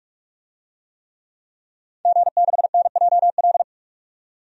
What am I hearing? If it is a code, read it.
Morse code, heard as a keyed tone, about 47 wpm, 700 Hz. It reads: G6NJL
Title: DXCC-100 Call Signs - 5 Characters - 45wpm